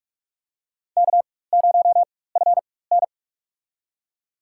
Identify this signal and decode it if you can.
Morse code, heard as a keyed tone, about 41 wpm, spaced as for 20 wpm, 700 Hz. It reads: K0FN
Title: USA Call Signs - 4 Characters - 4x Character Spacing - 45wpm